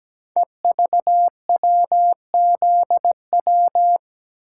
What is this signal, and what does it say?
Morse code, heard as a keyed tone, about 17 wpm, 700 Hz. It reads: EVWZW
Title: Code Groups - 5 Letter:Numbers - 17wpm